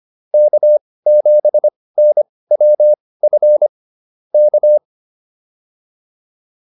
Morse code, heard as a keyed tone, about 25 wpm, spaced as for 18 wpm, 600 Hz. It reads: K7NWF K
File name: QSO209 18